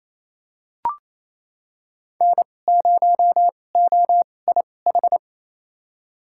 Morse code, heard as a keyed tone, about 28 wpm, spaced as for 20 wpm, 700 Hz. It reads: N0OIH
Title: USA Call Signs - 5 Characters - 2x Character Spacing - 28wpm